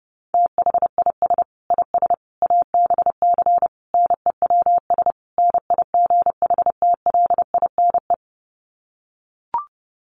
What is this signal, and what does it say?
Morse code, heard as a keyed tone, about 30 wpm, 700 Hz. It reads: THIS IS ABC NEWS NIGHTLINE
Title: Sets of 5 Words 30wpm